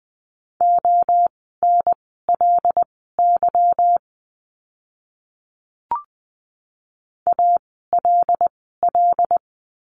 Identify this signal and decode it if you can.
Morse code, heard as a keyed tone, about 20 wpm, spaced as for 15 wpm, 700 Hz. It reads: ONLY ALL
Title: 192 Lesson 31f - Letter v - Words 20wpm